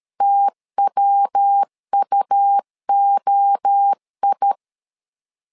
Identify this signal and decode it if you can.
Morse code, heard as a keyed tone, about 13 wpm, 800 Hz. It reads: TWUOI
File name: LEZIONI / 4 Q 12wpm 1833